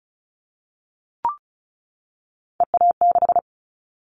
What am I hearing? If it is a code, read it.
Morse code, heard as a keyed tone, about 36 wpm, 700 Hz. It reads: EA6
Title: Callsign Prefixes for Top 100 DXCC - 35wpm